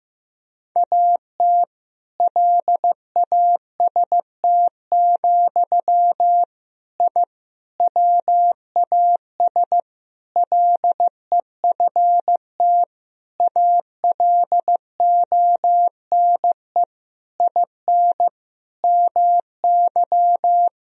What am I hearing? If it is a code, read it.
Morse code, heard as a keyed tone, about 15 wpm, 700 Hz. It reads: AT LAST, I WAS LEFT ALONE IN MY
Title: Sentences from Top 1000 Words - Review Part 1 - 15wpm